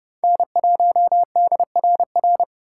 Morse code, heard as a keyed tone, about 30 wpm, 700 Hz. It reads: N1DRR
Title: USA Call Signs - 5 Characters - 2x Character Spacing - 30wpm